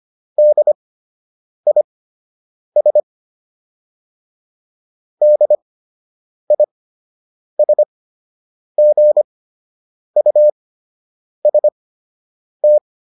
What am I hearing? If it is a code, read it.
Morse code, heard as a keyed tone, about 25 wpm, spaced as for 8 wpm, 600 Hz. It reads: DIS DISGUST